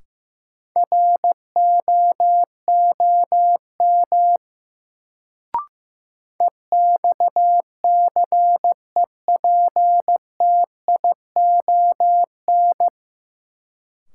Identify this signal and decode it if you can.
Morse code, heard as a keyed tone, about 16 wpm, 700 Hz. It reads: ROOM EXCEPTION
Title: Sets of 4 Words - New Word Drill - 15wpm